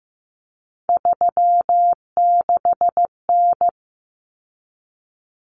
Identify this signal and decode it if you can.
Morse code, heard as a keyed tone, about 15 wpm, 700 Hz. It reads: 36N
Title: Common Cut Numbers - 3-Digit 15wpm